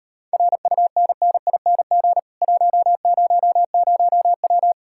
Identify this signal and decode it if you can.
Morse code, heard as a keyed tone, about 38 wpm, 700 Hz. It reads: RUNNING 100W